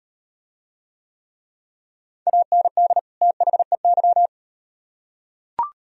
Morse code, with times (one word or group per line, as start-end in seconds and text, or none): 2.3-3.0 AND
3.2-4.3 THEY